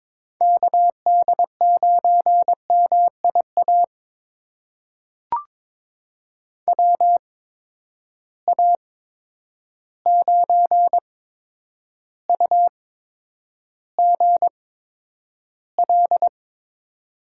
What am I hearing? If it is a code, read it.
Morse code, heard as a keyed tone, about 21 wpm, spaced as for 9 wpm, 700 Hz. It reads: KD9MIA WA9UGL
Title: USA Call Signs - 6 Characters - 8x Character Spacing - 22wpm